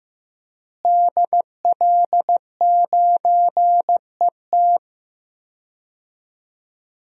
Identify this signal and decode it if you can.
Morse code, heard as a keyed tone, about 15 wpm, 700 Hz. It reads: DL9ET
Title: DXCC-100 Call Signs - 5 Characters - Repeated 3x - 15wpm